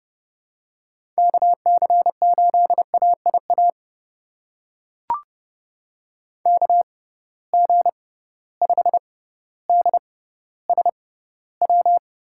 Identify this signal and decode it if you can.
Morse code, heard as a keyed tone, about 31 wpm, spaced as for 16 wpm, 700 Hz. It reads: KC8AIA KG5DSW
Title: USA Call Signs - 6 Characters - 6x Character Spacing - 30wpm